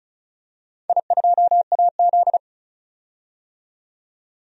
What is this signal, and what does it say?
Morse code, heard as a keyed tone, about 34 wpm, 700 Hz. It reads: I2AZ
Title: DXCC-100 Call Signs - 4 Characters - 35wpm